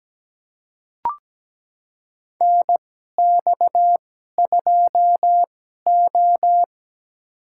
Morse code, heard as a keyed tone, about 17 wpm, spaced as for 12 wpm, 700 Hz. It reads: NX2O
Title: USA Call Signs - 2x Character Spacing - 17wpm